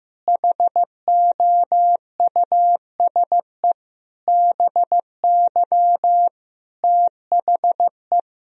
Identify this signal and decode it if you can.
Morse code, heard as a keyed tone, about 15 wpm, 700 Hz. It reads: HOUSE BY THE
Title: Sentences from Top 1000 Words - Review Part 2 - 15wpm